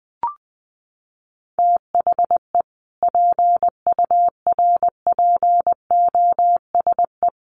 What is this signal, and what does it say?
Morse code, heard as a keyed tone, about 20 wpm, 700 Hz. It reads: THE PURPOSE
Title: Sets of 5 Words 20wpm